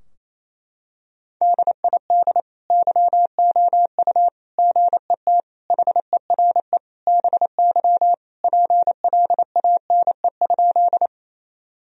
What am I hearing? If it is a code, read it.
Morse code, heard as a keyed tone, about 28 wpm, 700 Hz. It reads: DID YOU GET HERE BY PLANE?
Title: Sentences from Top 1000 Words - Review Part 3 - 28wpm